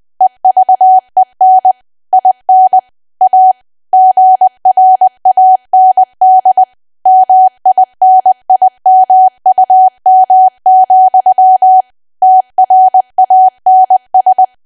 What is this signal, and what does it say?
Morse code, heard as a keyed tone, about 20 wpm, 750 Hz. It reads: EVEN IN A GRAND MINIMUM, TRANS
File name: cwa / W1AW-Level3 / 20WPM